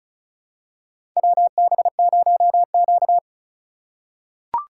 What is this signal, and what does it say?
Morse code, heard as a keyed tone, about 36 wpm, 700 Hz. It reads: WB0Q
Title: USA Call Signs - 4 Characters - 3x Character Spacing - 35wpm